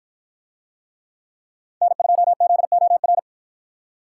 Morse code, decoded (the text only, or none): N2ZOR